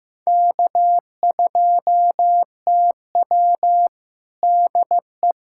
Morse code, heard as a keyed tone, about 15 wpm, 700 Hz. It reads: K2TW DE
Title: K1USN SST QSO Practice 15wpm